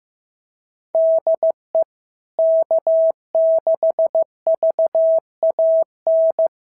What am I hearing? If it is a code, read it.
Morse code, heard as a keyed tone, about 16 wpm, 650 Hz. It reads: DE K6VAN
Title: SOTA QSO Practice 15wpm